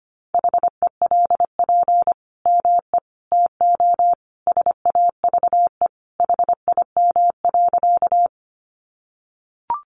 Morse code, heard as a keyed tone, about 25 wpm, 700 Hz. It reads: HELP ME TO SAVE HIM.
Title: Sentences from Top 1100 Words 25wpm